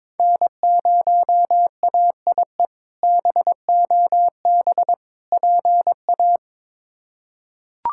Morse code, heard as a keyed tone, about 22 wpm, 700 Hz. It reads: N0AIE BOB PA
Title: K1USN SST QSO Practice - 22wpm